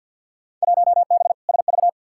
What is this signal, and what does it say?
Morse code, heard as a keyed tone, about 51 wpm, 700 Hz. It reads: 1B SV
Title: Field Day Exchanges Encore - 50wpm